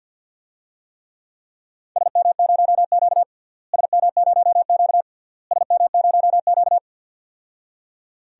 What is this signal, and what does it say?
Morse code, heard as a keyed tone, about 52 wpm, 700 Hz. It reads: SM0Q SM0Q SM0Q
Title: DXCC-100 Call Signs - 4 Characters - Repeated 3x - 50wpm